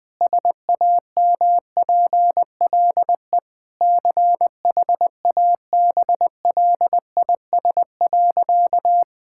Morse code, heard as a keyed tone, about 20 wpm, 700 Hz. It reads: SAMPLE CHABLIS.
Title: Pangram Challenge v2 - 20wpm